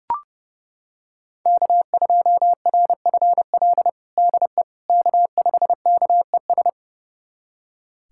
Morse code, heard as a keyed tone, about 30 wpm, 700 Hz. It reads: K2RFL DE K5KES
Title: USA Callsign DE Callsign - 5 Characters - 30wpm